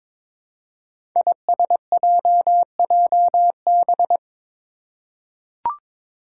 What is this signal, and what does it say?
Morse code, heard as a keyed tone, about 22 wpm, 700 Hz. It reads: ISJJB
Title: Five Challenging Letter-Number Combinations 22wpm